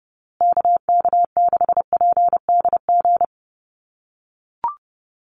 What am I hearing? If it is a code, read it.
Morse code, heard as a keyed tone, about 30 wpm, 700 Hz. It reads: KK6PDG
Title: USA Call Signs Repeated 3x 30wpm